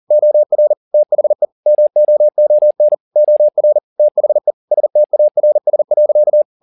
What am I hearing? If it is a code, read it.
Morse code, heard as a keyed tone, about 40 wpm, 600 Hz. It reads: OR THE MOON OR THE STARS.
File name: the old man and the sea 2 40wpm